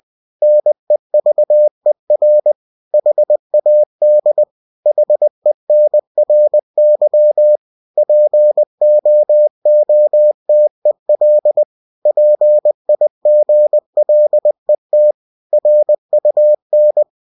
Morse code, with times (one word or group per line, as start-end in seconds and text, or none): 0.4-2.5 NEVER
2.9-4.4 HAD
4.9-7.6 HENRY
8.0-11.6 POOTEL
12.1-15.1 PIGLET
15.5-17.0 RUN